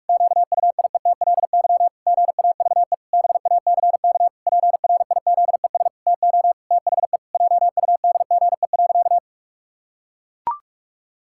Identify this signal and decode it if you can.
Morse code, heard as a keyed tone, about 42 wpm, 700 Hz. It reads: QUIETLY GAVE BACK PRIZES TO THE JUDGE.
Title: Pangram Challenge v2 - 45wpm